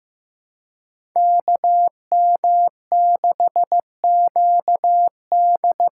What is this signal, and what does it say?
Morse code, heard as a keyed tone, about 15 wpm, 700 Hz. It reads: KM6QD
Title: USA Call Signs - 6 Characters - 6x Character Spacing - 15wpm